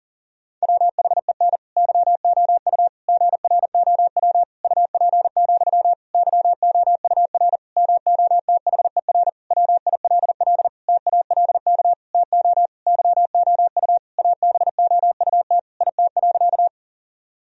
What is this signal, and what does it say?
Morse code, heard as a keyed tone, about 40 wpm, 700 Hz. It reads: WHEN YOU GROW UP, YOUR MOTHER WILL TALK TO YOU ABOUT IT.
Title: Sentences from Top 1000 Words - Review Part 1 - 40wpm